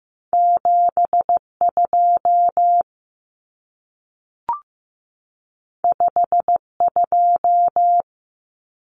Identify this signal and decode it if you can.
Morse code, heard as a keyed tone, about 15 wpm, 700 Hz. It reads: 72 52
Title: Common Cut Numbers - 2-Digit 15wpm